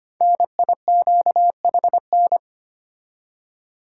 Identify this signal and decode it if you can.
Morse code, heard as a keyed tone, about 25 wpm, 700 Hz. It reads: NIQHN
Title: Code Groups - 5 Letter:Numbers - 25wpm